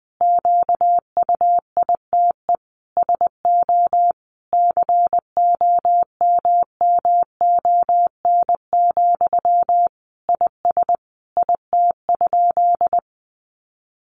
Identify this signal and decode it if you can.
Morse code, heard as a keyed tone, about 20 wpm, 700 Hz. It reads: QUITE SO COMMON, IS IT?
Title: Sentences from Top 700 Words 20wpm